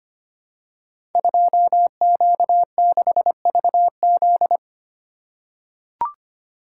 Morse code, heard as a keyed tone, about 25 wpm, 700 Hz. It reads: 2Q6VZ
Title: Code Groups - 5 Letter:Numbers - 25wpm